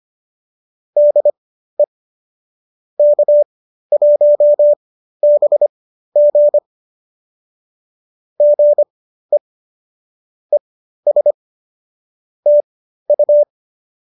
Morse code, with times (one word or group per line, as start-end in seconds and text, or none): 1.0-1.8 DE
3.0-6.6 K1BG
8.4-9.4 GE
10.5-11.3 ES
12.5-13.4 TU